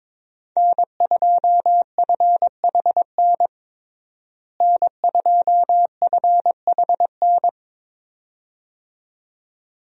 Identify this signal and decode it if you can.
Morse code, heard as a keyed tone, about 22 wpm, 700 Hz. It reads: N2FHN N2FHN
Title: USA Call Signs - 5 Characters - Repeated 2x - 22wpm